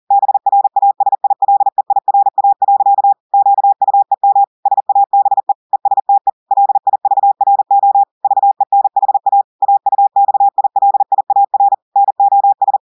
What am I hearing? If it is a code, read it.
Morse code, heard as a keyed tone, about 40 wpm, 800 Hz. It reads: BRASILEIRA. QUEM SABE ESTE LIVRO VENHA AUXILIAR NOS
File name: audios / portugais / portugais texte en clair 1 vitesse 40 mots minutes